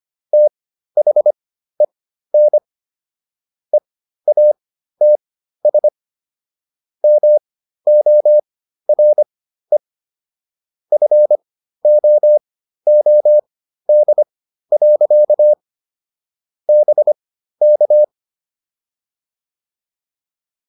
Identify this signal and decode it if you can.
Morse code, heard as a keyed tone, about 25 wpm, spaced as for 13 wpm, 600 Hz. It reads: THEN EATS MORE FOOD. BK